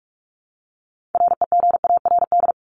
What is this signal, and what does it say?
Morse code, heard as a keyed tone, about 44 wpm, 700 Hz. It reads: REGARD
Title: ED Suffix - Top 100 Words with Base Words 45wpm